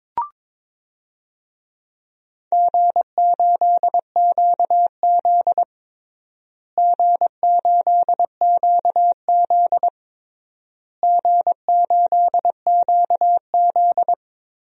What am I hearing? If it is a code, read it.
Morse code, heard as a keyed tone, about 22 wpm, 700 Hz. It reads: G8QZ G8QZ G8QZ
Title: DXCC-100 Call Signs - 4 Characters - Repeated 3x - 22wpm